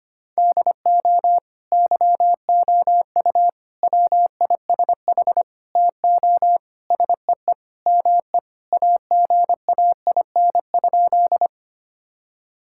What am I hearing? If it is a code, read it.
Morse code, heard as a keyed tone, about 25 wpm, 700 Hz. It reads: DO YOU WISH TO SEE ME AGAIN?
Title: Sentences from Top 1600 Words 25wpm